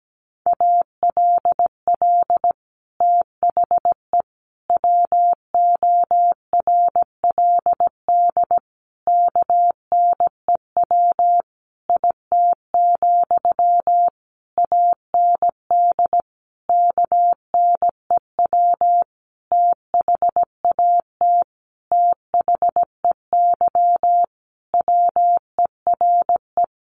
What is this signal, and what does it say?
Morse code, heard as a keyed tone, about 17 wpm, 700 Hz. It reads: ALL THE WORLD KNEW IT, AND KNEW THAT THEY WERE
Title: Sentences from Top 1200 Words 17wpm